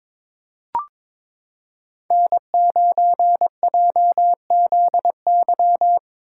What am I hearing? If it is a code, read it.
Morse code, heard as a keyed tone, about 22 wpm, 700 Hz. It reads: N9JZY
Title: K1USN SST QSO Practice - 22wpm